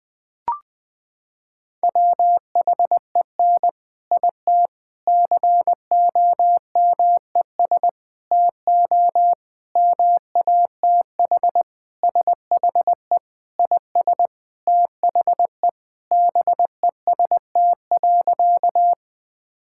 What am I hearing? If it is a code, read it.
Morse code, heard as a keyed tone, about 20 wpm, 700 Hz. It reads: WHEN IT COMES TO MATH SHE IS THE BEST.
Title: Sentences with 4-Word Phrases - 20wpm